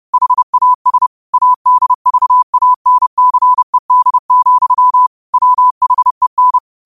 Morse code, heard as a keyed tone, about 30 wpm, 1000 Hz. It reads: HAS ADVANCED, WHEN
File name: qotdfiles / 30wpm